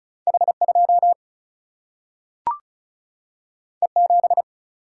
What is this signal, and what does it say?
Morse code, heard as a keyed tone, about 36 wpm, 700 Hz. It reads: H2 E7
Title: Single-Letter Single-Number 35wpm